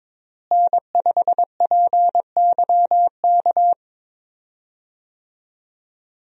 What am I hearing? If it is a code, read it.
Morse code, heard as a keyed tone, about 22 wpm, 700 Hz. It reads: N5PYK
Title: USA Call Signs - 5 Characters - Repeated 2x - 22wpm